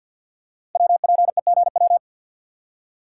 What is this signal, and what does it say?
Morse code, heard as a keyed tone, about 52 wpm, 700 Hz. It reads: WPEGW
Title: Code Groups - 5 Letter:Numbers - 50wpm